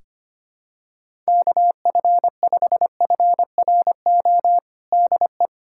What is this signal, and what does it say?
Morse code, heard as a keyed tone, about 25 wpm, 700 Hz. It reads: KF5FRO DE
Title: USA Callsign DE Callsign - 6 Characters - 25wpm